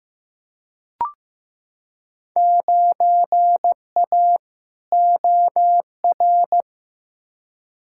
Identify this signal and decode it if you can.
Morse code, heard as a keyed tone, about 15 wpm, 700 Hz. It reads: 9A OR
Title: Field Day Exchanges Encore - 15wpm